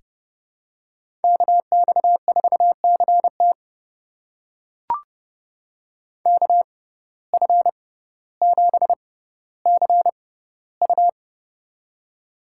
Morse code, heard as a keyed tone, about 31 wpm, spaced as for 15 wpm, 700 Hz. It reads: KX4CT KF7CU
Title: USA Call Signs - 5 Characters - 6x Character Spacing - 30wpm